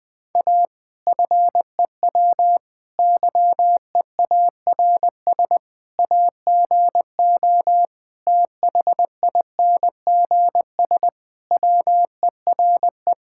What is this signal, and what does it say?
Morse code, heard as a keyed tone, about 20 wpm, 700 Hz. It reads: A FEW YEARS AGO THINGS WERE
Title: Sentences with 4-Word Phrases - 20wpm